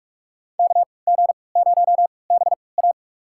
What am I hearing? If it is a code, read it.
Morse code, heard as a keyed tone, about 47 wpm, spaced as for 26 wpm, 700 Hz. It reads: KG0BA
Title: USA Call Signs - 3x Character Spacing - 45wpm